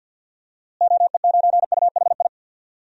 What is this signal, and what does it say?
Morse code, heard as a keyed tone, about 51 wpm, 700 Hz. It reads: OE9UHI
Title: DXCC-100 Call Signs - 6 Characters - Repeated 2x - 50wpm